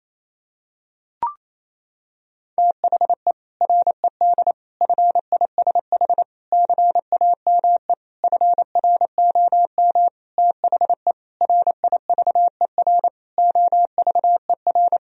Morse code, heard as a keyed tone, about 28 wpm, 700 Hz. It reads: THE RED FISH CAME FROM THE RIVER OVER
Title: Sentences from Top 1000 Words - Review Part 2 - 28wpm